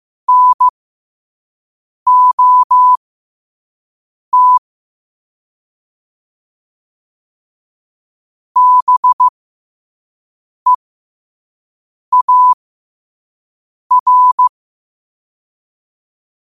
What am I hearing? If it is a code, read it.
Morse code, heard as a keyed tone, about 15 wpm, spaced as for 5 wpm, 1000 Hz. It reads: NOT BEAR
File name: qotdfiles / 5wpm